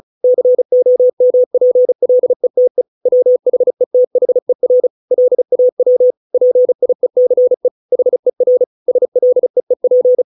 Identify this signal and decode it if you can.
Morse code, heard as a keyed tone, about 35 wpm, 500 Hz. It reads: COMPLETE WHETHER LAW PIECE HER SLEEP